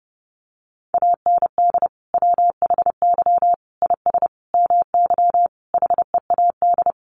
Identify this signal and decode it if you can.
Morse code, heard as a keyed tone, about 30 wpm, 700 Hz. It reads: AND WHY IS MY HEAD
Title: Sentences from Top 1100 Words 30wpm